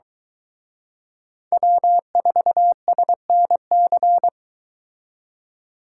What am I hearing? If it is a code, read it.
Morse code, heard as a keyed tone, about 23 wpm, 700 Hz. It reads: W4SNC